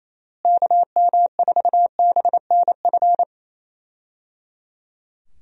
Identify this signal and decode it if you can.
Morse code, heard as a keyed tone, about 27 wpm, 700 Hz. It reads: KM4BNF
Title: USA Call Signs - 6 Characters - Repeated 3x - 28wpm